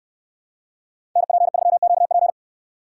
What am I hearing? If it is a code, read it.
Morse code, heard as a keyed tone, about 69 wpm, 700 Hz. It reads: NP3QP